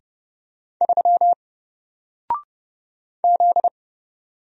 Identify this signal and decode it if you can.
Morse code, heard as a keyed tone, about 30 wpm, 700 Hz. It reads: 3 Z
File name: Single-Letter-Number-30wpm